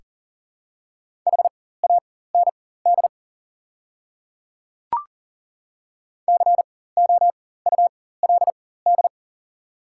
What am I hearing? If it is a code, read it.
Morse code, heard as a keyed tone, about 40 wpm, spaced as for 19 wpm, 700 Hz. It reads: HAND COULD
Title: 4x Character Spacing - Top 200 Words - 40wpm